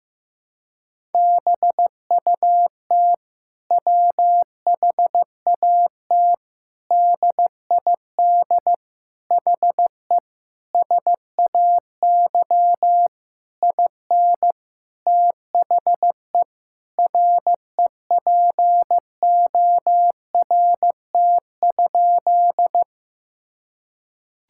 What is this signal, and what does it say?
Morse code, heard as a keyed tone, about 15 wpm, 700 Hz. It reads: BUT WHAT DID HE SAY IN THE REPORT?
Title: Sentences from Top 1000 Words - Review Part 6 - 15wpm